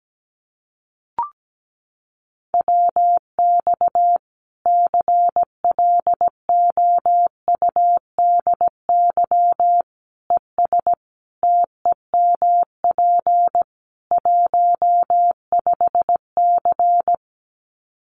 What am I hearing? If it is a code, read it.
Morse code, heard as a keyed tone, about 17 wpm, 700 Hz. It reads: WX CLOUDY ES TEMP 15C
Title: Common QSO Phrases 17wpm